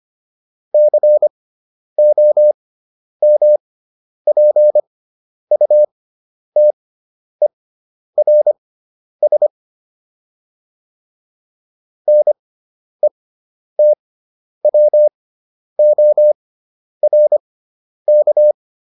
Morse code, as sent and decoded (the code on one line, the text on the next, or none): -.-. --- -- .--. ..- - . .-. ... / -. - .-- --- .-. -.-
COMPUTERS NETWORK